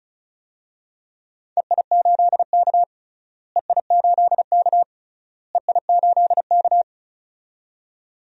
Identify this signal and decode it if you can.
Morse code, heard as a keyed tone, about 36 wpm, 700 Hz. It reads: EI8K EI8K EI8K
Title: DXCC-100 Call Signs - 4 Characters - Repeated 3x - 35wpm